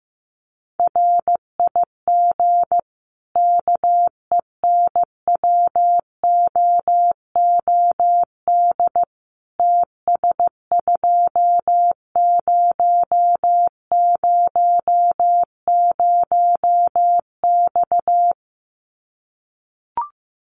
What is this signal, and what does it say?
Morse code, heard as a keyed tone, about 15 wpm, 700 Hz. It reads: RIG KENWOOD TS2000X
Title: Common QSO Phrases 15wpm